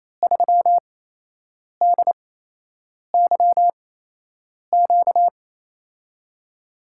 Morse code, heard as a keyed tone, about 28 wpm, spaced as for 8 wpm, 700 Hz. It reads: 3DYQ
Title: USA Call Signs - 6 Characters - 8x Character Spacing - 28wpm